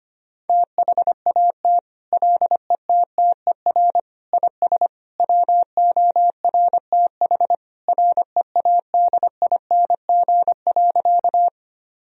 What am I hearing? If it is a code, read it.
Morse code, heard as a keyed tone, about 25 wpm, 700 Hz. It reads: THAT LETTER IS WORTH READING.
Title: Sentences from Top 1000 Words - Review Part 3 - 25wpm